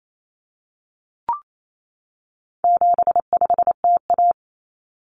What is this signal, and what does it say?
Morse code, heard as a keyed tone, about 28 wpm, 700 Hz. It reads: 75TA
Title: Common Cut Numbers - 4-Digit 28wpm